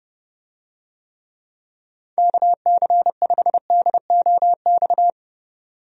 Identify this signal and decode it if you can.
Morse code, heard as a keyed tone, about 32 wpm, 700 Hz. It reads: KC5DOX
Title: USA Call Signs - 6 Characters - 30wpm